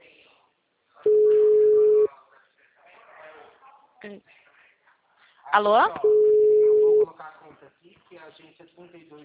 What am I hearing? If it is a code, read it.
Ringback tone (Europe, 425 Hz), with speech.